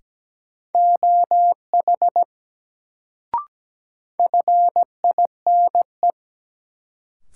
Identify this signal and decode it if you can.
Morse code, heard as a keyed tone, about 17 wpm, 700 Hz. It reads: OH FINE
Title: Sentences from Top 500 Words - New Word Drill - 17wpm